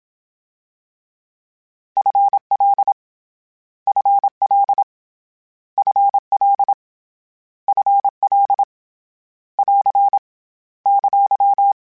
Morse code, heard as a keyed tone, about 27 wpm, 800 Hz. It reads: FL FL FL FL <AR> !
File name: Short Runs / Short Runs-mp3s / ShortContrasts @25